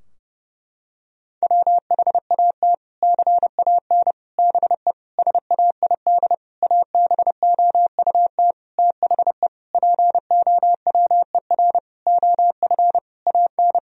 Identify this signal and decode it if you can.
Morse code, heard as a keyed tone, about 30 wpm, 700 Hz. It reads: WHAT CAN BE SAID ABOUT THE POWER OF AN
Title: Sentences from Top 1000 Words - Review Part 4 - 30wpm